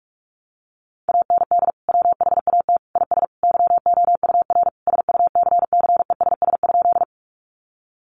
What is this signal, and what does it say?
Morse code, heard as a keyed tone, about 44 wpm, 700 Hz. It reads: AND WHAT IS YOUR SUCCESS?
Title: Sentences from Top 900 Words 45wpm